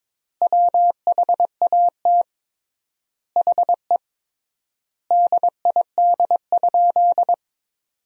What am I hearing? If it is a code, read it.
Morse code, heard as a keyed tone, about 22 wpm, 700 Hz. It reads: WHAT HE DID?
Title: 3x Word-Spacing - Sentences from Top 100 Words - 22wpm